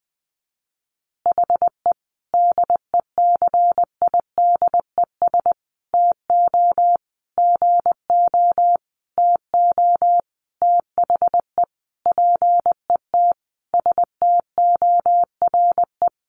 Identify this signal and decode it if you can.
Morse code, heard as a keyed tone, about 20 wpm, 700 Hz. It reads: HE DECIDES TO GO TO THE PET STORE
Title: Easy Phrases 20wpm